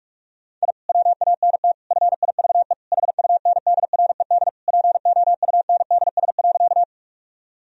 Morse code, heard as a keyed tone, about 46 wpm, 700 Hz. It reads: I WANT FIVE HUNDRED POUNDS.
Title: Sentences from Top 1000 Words - Review Part 2 - 45wpm